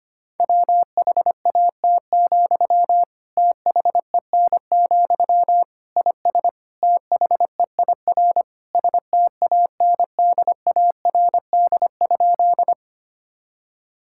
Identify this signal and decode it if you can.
Morse code, heard as a keyed tone, about 25 wpm, 700 Hz. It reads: WHAT, THEN, IS THEIR STANDARD?
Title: Sentences from Top 1000 Words - Review Part 6 - 25wpm